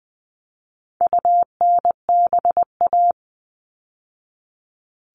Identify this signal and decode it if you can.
Morse code, heard as a keyed tone, about 20 wpm, 700 Hz. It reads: UNBA
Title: All Cut Numbers - 4-Digit 20wpm